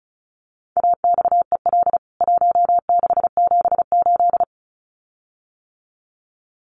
Morse code, heard as a keyed tone, about 34 wpm, 700 Hz. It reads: AXEL 1678
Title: CWT Exchange 35wpm